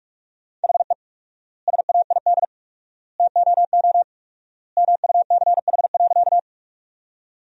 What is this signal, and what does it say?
Morse code, heard as a keyed tone, about 41 wpm, 700 Hz. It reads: HE SAID TOO MUCH.
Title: 4x Word-Spacing - Sentences from Top 200 Words - 45wpm